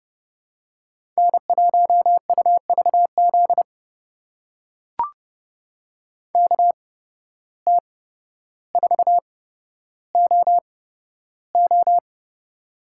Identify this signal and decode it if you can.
Morse code, heard as a keyed tone, about 31 wpm, spaced as for 13 wpm, 700 Hz. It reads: N1UVZ KT4OO
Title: USA Call Signs - 5 Characters - 8x Character Spacing - 30wpm